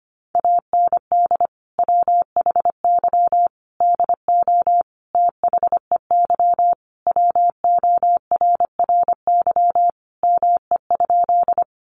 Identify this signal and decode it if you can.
Morse code, heard as a keyed tone, about 25 wpm, 700 Hz. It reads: AND WHY DO THEY WORRY ME?
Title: Sentences from Top 1100 Words 25wpm